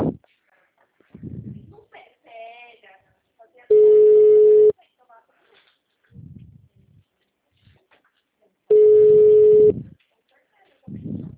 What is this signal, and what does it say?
Ringback tone (Europe, 425 Hz), with speech.
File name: out- -2045- - - .2881